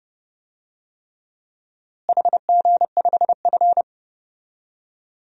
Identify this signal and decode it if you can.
Morse code, heard as a keyed tone, about 31 wpm, 700 Hz. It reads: HG5F
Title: DXCC-100 Call Signs - 4 Characters - Repeated 3x - 30wpm